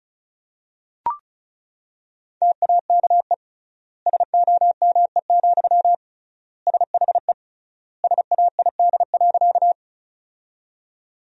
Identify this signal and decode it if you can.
Morse code, heard as a keyed tone, about 36 wpm, 700 Hz. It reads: TAKE SOME, SHE SAID.
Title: 3x Word-Spacing - Sentences from Top 200 Words - 35wpm